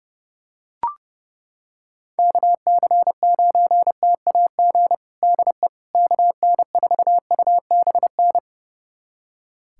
Morse code, heard as a keyed tone, about 32 wpm, 700 Hz. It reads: KC9TAG DE KN4UBN
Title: USA Callsign DE Callsign - 6 Characters - 30wpm